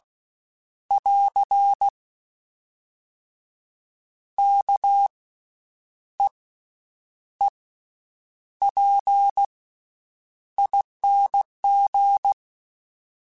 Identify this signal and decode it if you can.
Morse code, heard as a keyed tone, about 16 wpm, 800 Hz. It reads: <AR> K E E P ING
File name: Special INGs-mp3s / Special INGs-mp3s / Slow ing @15